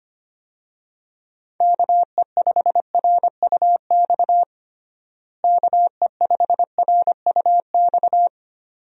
Morse code, heard as a keyed tone, about 25 wpm, 700 Hz. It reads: KE5RUX KE5RUX
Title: USA Call Signs - 6 Characters - Repeated 2x - 25wpm